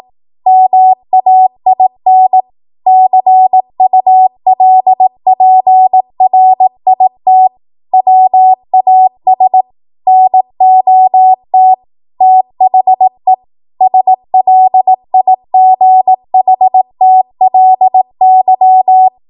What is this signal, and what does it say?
Morse code, heard as a keyed tone, about 18 wpm, 750 Hz. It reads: MAIN CULPRIT WAS NOT THE SLIGHTLY
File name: Morse / 18WPM